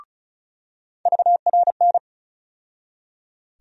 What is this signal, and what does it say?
Morse code, heard as a keyed tone, about 34 wpm, 700 Hz. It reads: VRN
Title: USA Call Sign Postfixes - 3 Letters 35wpm